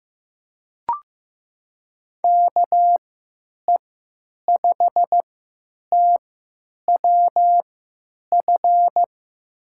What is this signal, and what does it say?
Morse code, heard as a keyed tone, about 15 wpm, spaced as for 9 wpm, 700 Hz. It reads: KE5TWF
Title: USA Call Signs - 6 Characters - 3x Character Spacing - 15wpm